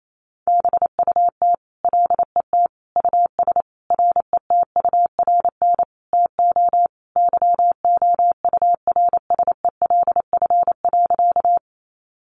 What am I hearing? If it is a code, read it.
Morse code, heard as a keyed tone, about 28 wpm, 700 Hz. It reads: BUT LET US RETURN TO YOURSELF.
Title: Sentences from Top 800 Words 28wpm